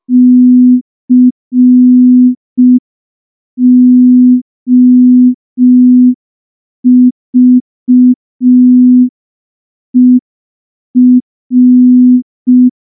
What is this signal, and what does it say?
Morse code, heard as a keyed tone, about 5 wpm, 250 Hz. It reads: COVER
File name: morse / morse-training-05WPM